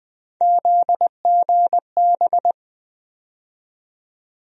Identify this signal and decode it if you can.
Morse code, heard as a keyed tone, about 20 wpm, 700 Hz. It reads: ZGB
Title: USA Call Sign Postfixes - 3 Letters 20wpm